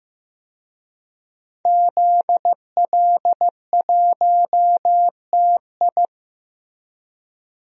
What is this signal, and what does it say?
Morse code, heard as a keyed tone, about 15 wpm, 700 Hz. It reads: ZL1TI
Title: DXCC-100 Call Signs - 5 Characters - Repeated 2x - 15wpm